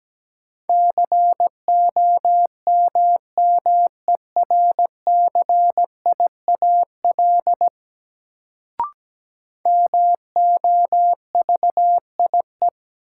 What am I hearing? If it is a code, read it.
Morse code, heard as a keyed tone, about 17 wpm, 700 Hz. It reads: COMMERCIAL MOVIE
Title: Sentences from Top 1000 Words - Review Part 5 - New Word Drill - 17wpm